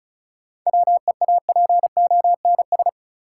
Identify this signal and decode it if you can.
Morse code, heard as a keyed tone, about 35 wpm, 700 Hz. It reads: WEAPONS
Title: Sets of 5 Words Encore - New Word Drill - 35wpm